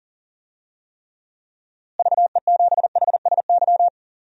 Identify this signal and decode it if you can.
Morse code, heard as a keyed tone, about 40 wpm, 700 Hz. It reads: VE7HSY
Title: DXCC-100 Call Signs - 6 Characters - Repeated 2x - 40wpm